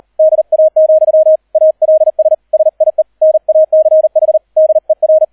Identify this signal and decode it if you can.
Morse code, heard as a keyed tone, about 38 wpm, 630 Hz, with radio noise. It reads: DA, ALS SIE NACH DER